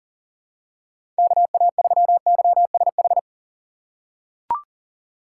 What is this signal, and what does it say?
Morse code, heard as a keyed tone, about 37 wpm, 700 Hz. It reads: KA3YSH
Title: USA Call Signs - 6 Characters - 2x Character Spacing - 40wpm